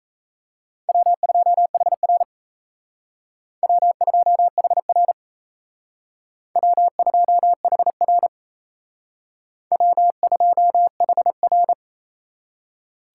Morse code, heard as a keyed tone, about 37 wpm, 700 Hz. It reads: W2HR W2HR W2HR W2HR